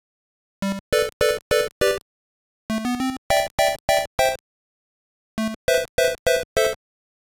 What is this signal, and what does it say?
There is no code.